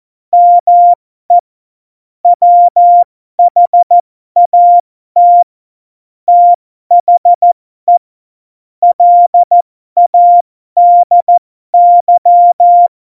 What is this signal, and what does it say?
Morse code, heard as a keyed tone, about 14 wpm, spaced as for 12 wpm, 700 Hz. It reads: ME WHAT THE LADY
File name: Morsekurs / Witze